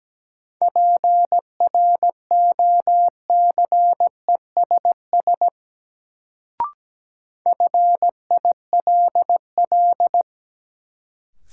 Morse code, heard as a keyed tone, about 17 wpm, 700 Hz. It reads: PROCESS FILL
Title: Sentences from Top 1000 Words - Review Part 3 - New Word Drill - 17wpm